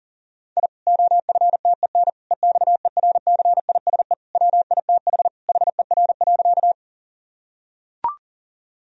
Morse code, heard as a keyed tone, about 40 wpm, 700 Hz. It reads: I OFTEN EXERCISE WITH HER.
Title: Sentences from Top 1000 Words - Review Part 4 - 40wpm